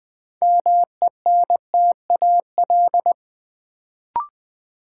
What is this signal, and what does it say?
Morse code, heard as a keyed tone, about 20 wpm, 700 Hz. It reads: MENTAL
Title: Sentences from Top 1000 Words - Review Part 4 - New Word Drill - 20wpm